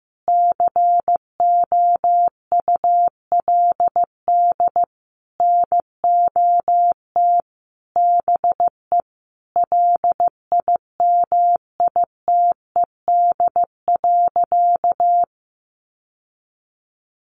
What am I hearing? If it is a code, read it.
Morse code, heard as a keyed tone, about 15 wpm, 700 Hz. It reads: COULD NOT BE LIMITED.
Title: Sentences from Top 900 Words 15wpm